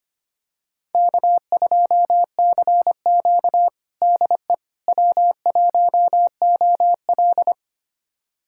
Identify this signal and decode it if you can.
Morse code, heard as a keyed tone, about 25 wpm, 700 Hz. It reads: K2CQ DE W1OL
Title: USA Callsign DE Callsign - 4 Characters - 25wpm